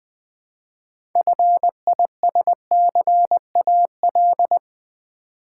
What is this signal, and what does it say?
Morse code, heard as a keyed tone, about 21 wpm, 700 Hz. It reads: FISCAL
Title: 6 Letter Words - First 3 Letters Given - 20wpm